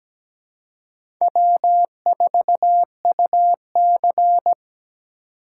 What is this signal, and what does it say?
Morse code, heard as a keyed tone, about 17 wpm, 700 Hz. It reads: W4UC
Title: USA Call Signs - 4 Characters - 2x Character Spacing - 17wpm